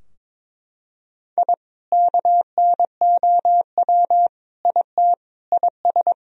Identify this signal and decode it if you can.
Morse code, heard as a keyed tone, about 22 wpm, 700 Hz. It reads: I KNOW IT IS
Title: Sentences from Top 1000 Words - Review Part 3 - 22wpm